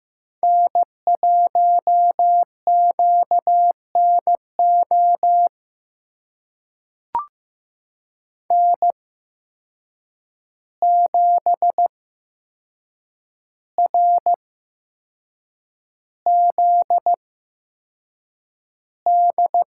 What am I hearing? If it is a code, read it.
Morse code, heard as a keyed tone, about 16 wpm, spaced as for 6 wpm, 700 Hz. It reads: N1QNO N7RZD
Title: USA Call Signs - 5 Characters - 8x Character Spacing - 15wpm